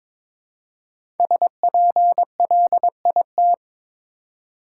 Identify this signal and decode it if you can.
Morse code, heard as a keyed tone, about 22 wpm, 700 Hz. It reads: SPLIT
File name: 5 Letter Words - First 3 Letters Given - 22wpm